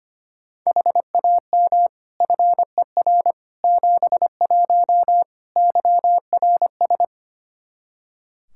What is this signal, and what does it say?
Morse code, heard as a keyed tone, about 25 wpm, 700 Hz. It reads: HAM FER 71 YRS
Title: Practice QSO - Exchange 3 - More Info - 25wpm